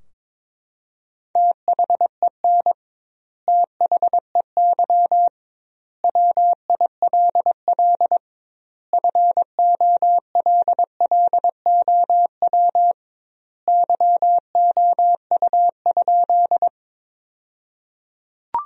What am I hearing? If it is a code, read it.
Morse code, heard as a keyed tone, about 22 wpm, 700 Hz. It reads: THEN THEY WILL FOLLOW YOU?
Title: 2x Word-Spacing - Sentences from Top 200 Words - 22wpm